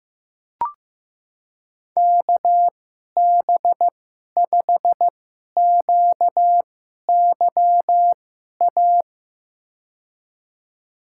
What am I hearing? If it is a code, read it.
Morse code, heard as a keyed tone, about 15 wpm, spaced as for 11 wpm, 700 Hz. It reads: KB5QYA
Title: USA Call Signs - 6 Characters - 2x Character Spacing - 15wpm